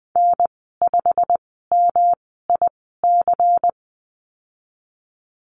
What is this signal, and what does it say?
Morse code, heard as a keyed tone, about 20 wpm, spaced as for 15 wpm, 700 Hz. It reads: N5MIC
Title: 186 Lesson 30g - Letter b - Call Signs 20wpm